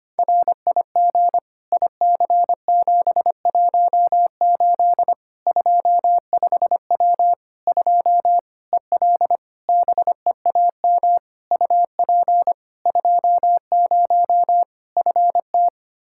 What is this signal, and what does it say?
Morse code, heard as a keyed tone, about 25 wpm, 700 Hz. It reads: RIG IC718 25W 2 EL BEAM UP 20 FT
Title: Practice QSO - Exchange 2 - Info - 25wpm